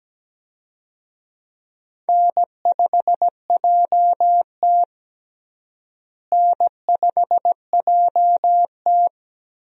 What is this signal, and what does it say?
Morse code, heard as a keyed tone, about 17 wpm, 700 Hz. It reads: N5JT N5JT
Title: USA Call Signs - 4 Characters - Repeated 3x - 17wpm